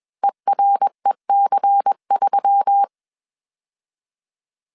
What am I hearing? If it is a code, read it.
Morse code, heard as a keyed tone, about 21 wpm, 800 Hz. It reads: EREC3
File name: CW / DET / 1 6 1500C S C 20wpm 2116